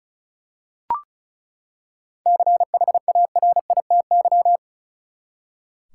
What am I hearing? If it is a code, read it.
Morse code, heard as a keyed tone, about 36 wpm, 700 Hz. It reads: CHARITY
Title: Seven Letter Words - 35wpm